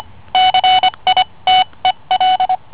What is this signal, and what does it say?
Morse code, heard as a keyed tone, about 25 wpm, spaced as for 21 wpm, 740 Hz. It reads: CITEL